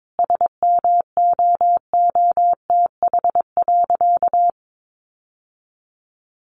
Morse code, heard as a keyed tone, about 22 wpm, 700 Hz. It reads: SMOOTH.
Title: Sentences from Top 1400 Words 22wpm